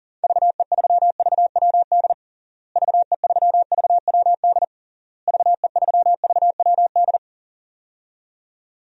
Morse code, heard as a keyed tone, about 37 wpm, 700 Hz. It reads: VE3VWD VE3VWD VE3VWD
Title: DXCC-100 Call Signs - 6 Characters - Repeated 3x - 40wpm